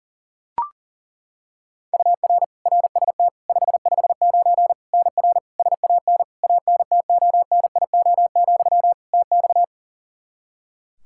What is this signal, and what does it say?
Morse code, heard as a keyed tone, about 37 wpm, 700 Hz. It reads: UR RST 559 NR SAN ANTONIO, TX
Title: K1USN SST QSO Practice 40wpm